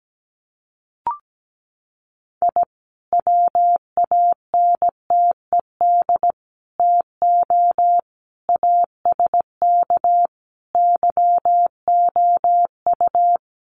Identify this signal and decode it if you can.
Morse code, heard as a keyed tone, about 17 wpm, 700 Hz. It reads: I WANTED TO ASK YOU
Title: Sets of 5 Words Encore2 17wpm